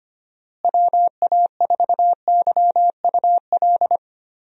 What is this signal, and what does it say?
Morse code, heard as a keyed tone, about 25 wpm, 700 Hz. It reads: WA4YUL
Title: USA Call Signs - 6 Characters - Repeated 3x - 25wpm